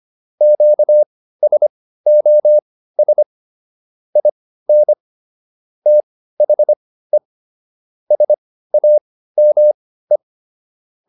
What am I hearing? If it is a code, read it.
Morse code, heard as a keyed tone, about 25 wpm, spaced as for 15 wpm, 600 Hz. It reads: QSOS IN THE SAME